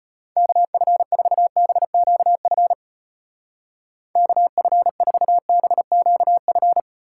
Morse code, heard as a keyed tone, about 33 wpm, 700 Hz. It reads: KF4BQF KF4BQF